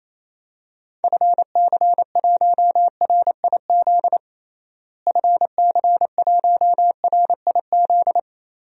Morse code, heard as a keyed tone, about 28 wpm, 700 Hz. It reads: FC1RIZ FC1RIZ
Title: DXCC-100 Call Signs - 6 Characters - Repeated 2x - 28wpm